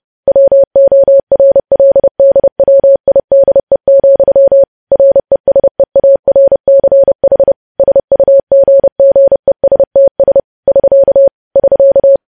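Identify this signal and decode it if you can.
Morse code, heard as a keyed tone, about 30 wpm, 550 Hz. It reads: WORLDWIDE, RESEARCH SUGGESTS <SK> <SK>
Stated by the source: Morse news headlines at 30 wpm.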